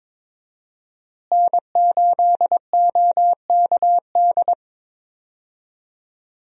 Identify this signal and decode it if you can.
Morse code, heard as a keyed tone, about 22 wpm, 700 Hz. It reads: N8OKD
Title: USA Call Signs - 5 Characters - Repeated 2x - 22wpm